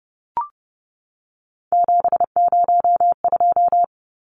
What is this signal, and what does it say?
Morse code, heard as a keyed tone, about 30 wpm, 700 Hz. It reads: 702
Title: Numbers - 3-Digit 30wpm